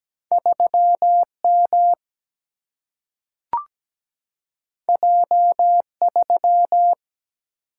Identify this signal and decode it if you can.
Morse code, heard as a keyed tone, about 17 wpm, 700 Hz. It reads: 3M J3
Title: Single-Letter Single-Number 17wpm